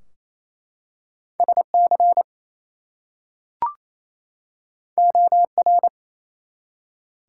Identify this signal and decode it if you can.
Morse code, heard as a keyed tone, about 27 wpm, 700 Hz. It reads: SC OR
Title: US States and Canadian Provinces 28wpm